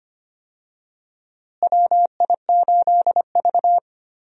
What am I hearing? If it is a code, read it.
Morse code, heard as a keyed tone, about 25 wpm, 700 Hz. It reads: WI8V
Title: USA Call Signs - 4 Characters - Repeated 3x - 25wpm